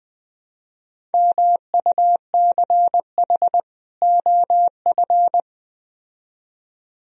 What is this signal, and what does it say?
Morse code, heard as a keyed tone, about 20 wpm, 700 Hz. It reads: MUCH OF